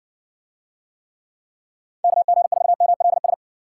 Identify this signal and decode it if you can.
Morse code, heard as a keyed tone, about 60 wpm, 700 Hz. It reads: KG4MLS